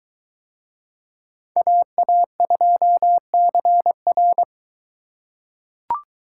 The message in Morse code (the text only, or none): AA2CR